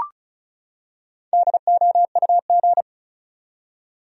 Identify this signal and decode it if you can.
Morse code, heard as a keyed tone, about 36 wpm, 700 Hz. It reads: DOUG
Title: USA Common Names - 35wpm